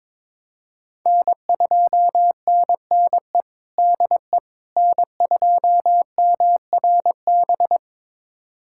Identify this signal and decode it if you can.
Morse code, heard as a keyed tone, about 22 wpm, 700 Hz. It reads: N2NNE DE N2MRB
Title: USA Callsign DE Callsign - 5 Characters - 22wpm